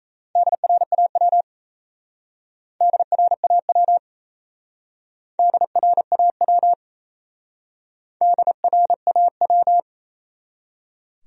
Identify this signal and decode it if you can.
Morse code, heard as a keyed tone, about 38 wpm, 700 Hz. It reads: DRAW DRAW DRAW DRAW